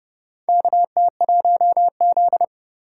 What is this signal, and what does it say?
Morse code, heard as a keyed tone, about 30 wpm, 700 Hz. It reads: KT1Z
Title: USA Call Signs - 4 Characters - 4x Character Spacing - 30wpm